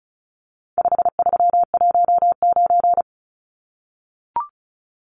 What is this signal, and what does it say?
Morse code, heard as a keyed tone, about 34 wpm, 700 Hz. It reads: 5319
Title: Numbers - 4-Digit 35wpm